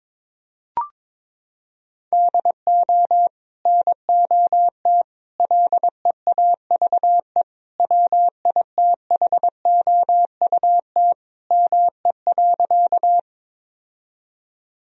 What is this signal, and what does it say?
Morse code, heard as a keyed tone, about 22 wpm, 700 Hz. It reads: DO NOT LEAVE WITHOUT ME.
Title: Sentences from Top 1000 Words - Review Part 3 - 22wpm